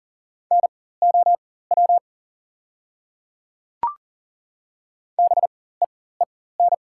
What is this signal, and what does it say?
Morse code, heard as a keyed tone, about 39 wpm, spaced as for 19 wpm, 690 Hz. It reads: NOW BEEN
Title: 4x Character Spacing - Top 100 Words - 40wpm